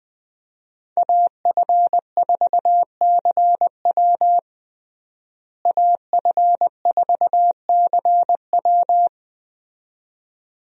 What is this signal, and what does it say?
Morse code, heard as a keyed tone, about 20 wpm, 700 Hz. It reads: AF4CW AF4CW
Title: USA Call Signs - 5 Characters - Repeated 2x - 20wpm